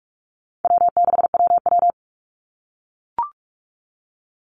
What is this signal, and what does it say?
Morse code, heard as a keyed tone, about 44 wpm, 700 Hz. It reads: W6WW
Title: NA SOTA Chaser Callsigns 45wpm